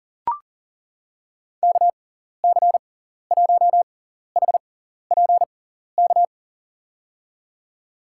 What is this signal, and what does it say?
Morse code, heard as a keyed tone, about 40 wpm, spaced as for 14 wpm, 700 Hz. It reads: KC1HPK
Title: USA Call Signs - 6 Characters - 6x Character Spacing - 40wpm